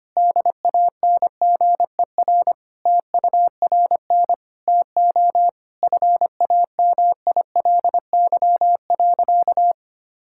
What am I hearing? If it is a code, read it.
Morse code, heard as a keyed tone, about 25 wpm, 700 Hz. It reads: DANGER TURN TO FAMILY.
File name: practice / 25wpm / Sentences with 3-Word Phrases - 25wpm